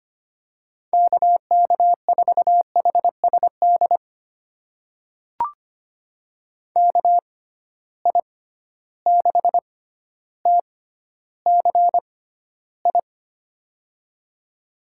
Morse code, heard as a keyed tone, about 25 wpm, spaced as for 13 wpm, 700 Hz. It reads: KK4HSD KI6TCI
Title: USA Call Signs - 6 Characters - 6x Character Spacing - 25wpm